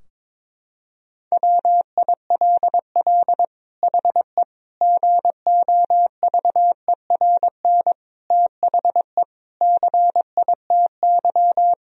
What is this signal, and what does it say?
Morse code, heard as a keyed tone, about 22 wpm, 700 Hz. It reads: WILL HE GOVERN THE CITY
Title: Sentences from Top 1000 Words - Review Part 2 - 22wpm